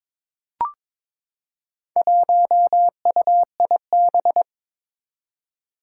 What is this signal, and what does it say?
Morse code, heard as a keyed tone, about 22 wpm, 700 Hz. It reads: 1UIB
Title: Four Challenging Letter-Number Combinations 22wpm